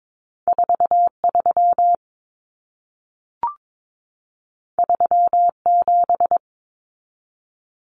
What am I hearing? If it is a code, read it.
Morse code, heard as a keyed tone, about 22 wpm, 700 Hz. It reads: 43 37
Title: Common Cut Numbers - 2-Digit 22wpm